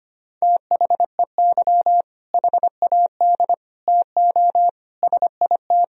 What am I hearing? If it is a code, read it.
Morse code, heard as a keyed tone, about 25 wpm, 700 Hz. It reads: THEY HAD TO SIT
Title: Sentences from Top 1000 Words - Review Part 1 - 25wpm